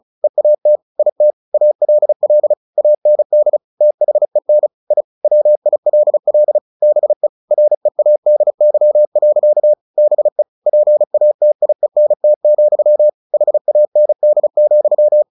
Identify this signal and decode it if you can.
Morse code, heard as a keyed tone, about 34 wpm, 600 Hz. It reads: EAT IT ALL AND THEN I WILL BE READY. BE PATIENT, HAND,
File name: the old man and the sea / the old man and the sea 1 35wpm